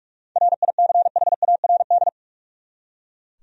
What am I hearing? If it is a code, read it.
Morse code, heard as a keyed tone, about 42 wpm, 700 Hz. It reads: RICHARD
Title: USA Common Names - 45wpm